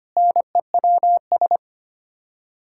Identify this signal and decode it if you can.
Morse code, heard as a keyed tone, about 25 wpm, 700 Hz. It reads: NEWS
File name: Sets of 9 Words - 25wpm